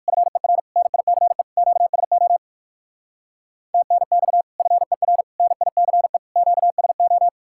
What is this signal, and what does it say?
Morse code, heard as a keyed tone, about 59 wpm, 700 Hz. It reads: FER NICE QSO TNX FER NICE QSO